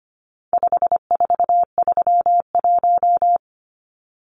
Morse code, heard as a keyed tone, about 25 wpm, 700 Hz. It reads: 5431
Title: Numbers - 4-Digit 25wpm